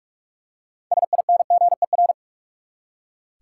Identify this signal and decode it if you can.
Morse code, heard as a keyed tone, about 46 wpm, 700 Hz. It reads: SINGER
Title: Six Letter Words - 45wpm